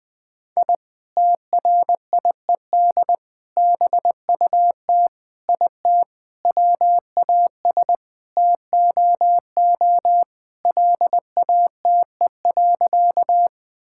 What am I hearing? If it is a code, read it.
Morse code, heard as a keyed tone, about 20 wpm, 700 Hz. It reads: I TRIED BUT IT WAS TOO LATE.
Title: Sentences with 3-Word Phrases - 20wpm